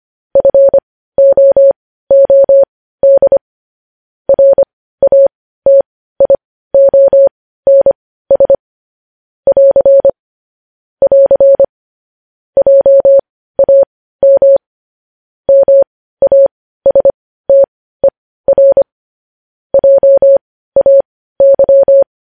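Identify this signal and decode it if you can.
Morse code, heard as a keyed tone, about 25 wpm, spaced as for 15 wpm, 550 Hz. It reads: FOOD RATIONS <AR> <AR> JAM MASTER JAY
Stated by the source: Morse news headlines sent at 15 wpm.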